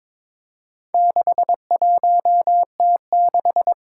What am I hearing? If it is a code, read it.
Morse code, heard as a keyed tone, about 21 wpm, 700 Hz. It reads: 61T6
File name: practice / 22wpm / Four Letter-Number Combinations - 22wpm